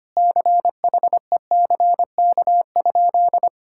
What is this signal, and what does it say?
Morse code, heard as a keyed tone, about 25 wpm, 700 Hz. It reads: CHECK?
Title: Sentences from Top 1000 Words - Review Part 6 - 25wpm